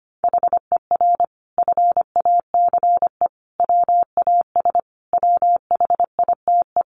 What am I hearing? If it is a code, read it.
Morse code, heard as a keyed tone, about 25 wpm, 700 Hz. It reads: HER FACE WAS WHITE
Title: Sentences from Top 1200 Words 25wpm